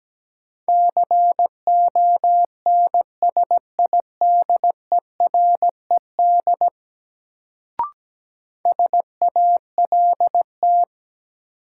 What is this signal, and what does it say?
Morse code, heard as a keyed tone, about 17 wpm, 700 Hz. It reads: CONSIDERED SALT
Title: Sets of 5 Words - New Word Drill - 17wpm